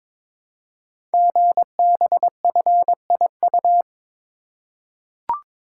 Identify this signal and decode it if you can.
Morse code, heard as a keyed tone, about 22 wpm, 700 Hz. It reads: GBFIU
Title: Code Groups - 5 Letters - 22wpm